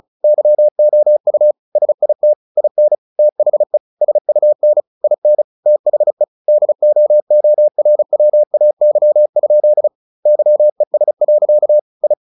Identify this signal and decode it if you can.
Morse code, heard as a keyed tone, about 35 wpm, 600 Hz. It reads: YOU SIT IN THE SUN IN THE DOORWAY? YES. I
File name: the old man and the sea / the old man and the sea 1 35wpm